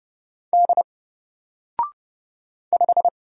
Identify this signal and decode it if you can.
Morse code, heard as a keyed tone, about 31 wpm, 700 Hz. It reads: D 5
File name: Single-Letter-Number-30wpm